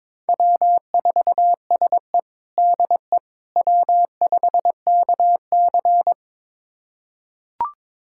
Morse code, heard as a keyed tone, about 22 wpm, 700 Hz. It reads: W4SE DE W5KC
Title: USA Callsign DE Callsign - 4 Characters - 22wpm